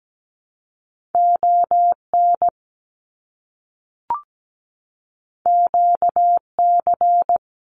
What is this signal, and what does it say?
Morse code, heard as a keyed tone, about 17 wpm, 700 Hz. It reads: ON QC
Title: Canadian Provinces and Territories 17wpm